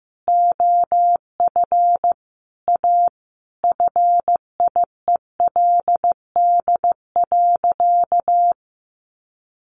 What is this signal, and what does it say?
Morse code, heard as a keyed tone, about 15 wpm, 700 Hz. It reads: OF A FIELD.
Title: Sentences from Top 1000 Words 15wpm